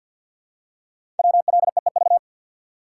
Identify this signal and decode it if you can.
Morse code, heard as a keyed tone, about 52 wpm, 700 Hz. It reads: WLEEV
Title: Code Groups - 5 Letters - 50wpm